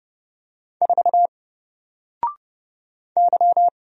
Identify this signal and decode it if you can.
Morse code, heard as a keyed tone, about 29 wpm, 700 Hz. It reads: 4 Y
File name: Single-Letter-Number-30wpm